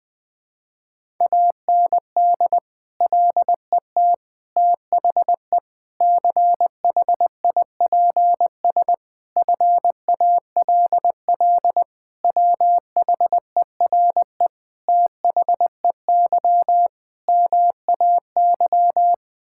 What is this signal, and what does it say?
Morse code, heard as a keyed tone, about 20 wpm, 700 Hz. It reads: AND LET THE CHIPS FALL WHERE THEY MAY
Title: Sets of 8 Words - 20wpm